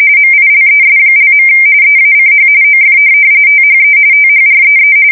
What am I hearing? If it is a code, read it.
RTTY, about 45.45 baud. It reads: MILES). EACH OF THE THREE
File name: fsk170 / USS-New-Jersey